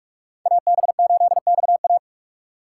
Morse code, heard as a keyed tone, about 46 wpm, 700 Hz. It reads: AB8XA
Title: USA Call Signs - 5 Characters - 3x Character Spacing - 45wpm